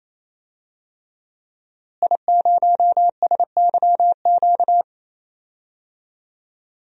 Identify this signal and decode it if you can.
Morse code, heard as a keyed tone, about 28 wpm, 700 Hz. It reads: I0SYQ
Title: DXCC-100 Call Signs - 5 Characters - Repeated 3x - 28wpm